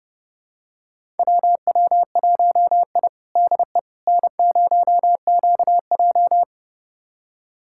Morse code, heard as a keyed tone, about 30 wpm, 700 Hz. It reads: WW1I DE N0QJ
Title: USA Callsign DE Callsign - 4 Characters - 30wpm